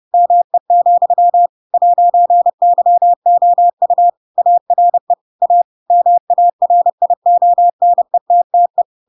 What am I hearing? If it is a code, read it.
Morse code, heard as a keyed tone, about 30 wpm, 700 Hz. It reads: ME, 'YOU ARE A MARIONETTE
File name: Pinocchio / Pinocchio ch7 30 30 ik1hge